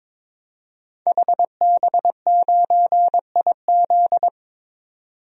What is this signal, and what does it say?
Morse code, heard as a keyed tone, about 22 wpm, 700 Hz. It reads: HB9IZ
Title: DXCC-100 Call Signs - 5 Characters - 22wpm